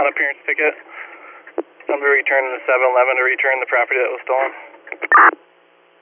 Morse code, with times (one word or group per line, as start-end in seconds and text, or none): none